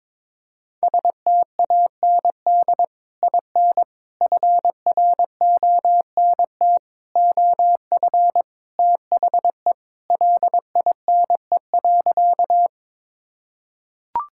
Morse code, heard as a keyed tone, about 22 wpm, 700 Hz. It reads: STAND IN FRONT OF THE LINE.
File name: practice / 22wpm / Sentences with 3-Word Phrases - 22wpm